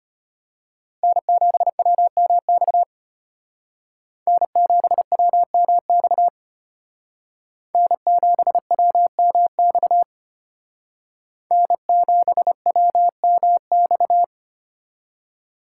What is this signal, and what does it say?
Morse code, heard as a keyed tone, about 32 wpm, 700 Hz. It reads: N7WMX N7WMX N7WMX N7WMX